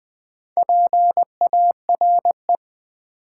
Morse code, heard as a keyed tone, about 20 wpm, 700 Hz. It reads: PARE
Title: Build A Word - 4 Letters At A Time - 8-Letter Words - 20wpm